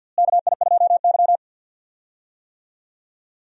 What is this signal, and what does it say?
Morse code, heard as a keyed tone, about 52 wpm, 700 Hz. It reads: KI2Y
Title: USA Call Signs - 4 Characters - Repeated 2x - 50wpm